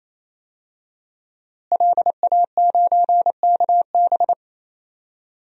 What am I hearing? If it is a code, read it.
Morse code, heard as a keyed tone, about 28 wpm, 700 Hz. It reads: LA9KB
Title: DXCC-100 Call Signs - 5 Characters - Repeated 3x - 28wpm